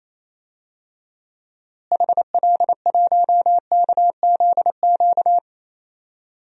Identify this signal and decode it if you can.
Morse code, heard as a keyed tone, about 28 wpm, 700 Hz. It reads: HL1KZQ
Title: DXCC-100 Call Signs - 6 Characters - Repeated 2x - 28wpm